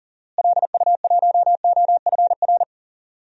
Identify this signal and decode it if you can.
Morse code, heard as a keyed tone, about 37 wpm, 700 Hz. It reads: LU1OFR
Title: DXCC-100 Call Signs - 6 Characters - Repeated 3x - 40wpm